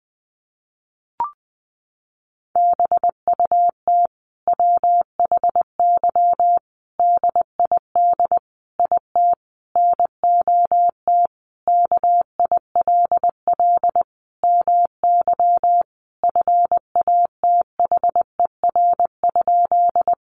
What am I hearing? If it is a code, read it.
Morse code, heard as a keyed tone, about 20 wpm, 700 Hz. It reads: BUT WHY DID IT NOT KILL MY FATHER?
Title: Sentences from Top 1400 Words 20wpm